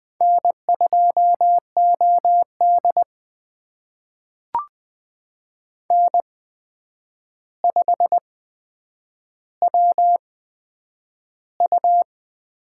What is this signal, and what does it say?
Morse code, heard as a keyed tone, about 20 wpm, spaced as for 9 wpm, 700 Hz. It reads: N2OD N5WU
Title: USA Call Signs - 4 Characters - 8x Character Spacing - 20wpm